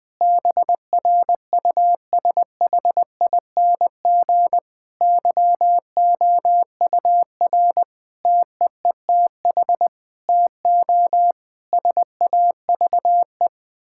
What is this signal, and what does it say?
Morse code, heard as a keyed tone, about 21 wpm, 700 Hz. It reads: BRUSHING YOUR TEETH TO SAVE
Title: Difficult Sentences with Hints - 20wpm